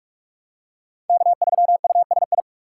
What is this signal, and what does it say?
Morse code, heard as a keyed tone, about 44 wpm, 700 Hz. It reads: K3USI